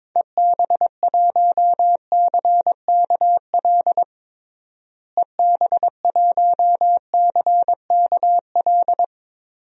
Morse code, heard as a keyed tone, about 22 wpm, 700 Hz. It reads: EB1CKL EB1CKL
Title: DXCC-100 Call Signs - 6 Characters - Repeated 2x - 22wpm